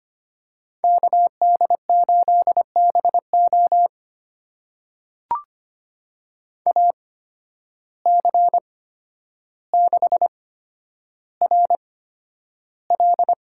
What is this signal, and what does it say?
Morse code, heard as a keyed tone, about 25 wpm, spaced as for 11 wpm, 700 Hz. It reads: KD8BO AC6RL
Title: USA Call Signs - 5 Characters - 8x Character Spacing - 25wpm